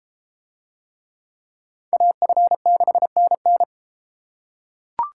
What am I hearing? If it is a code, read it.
Morse code, heard as a keyed tone, about 30 wpm, 700 Hz. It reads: AF6NN